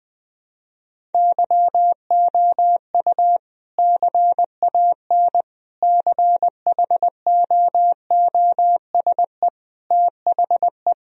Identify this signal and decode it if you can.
Morse code, heard as a keyed tone, about 21 wpm, 700 Hz. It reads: YOU CAN CHOOSE THE
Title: Sentences from Top 1000 Words - Review Part 4 - 20wpm